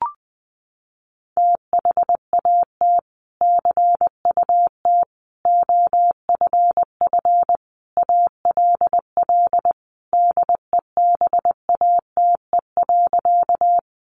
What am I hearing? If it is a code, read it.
Morse code, heard as a keyed tone, about 20 wpm, 700 Hz. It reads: THAT CUT OFF ALL DEBATE.
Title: Sentences from Top 1100 Words 20wpm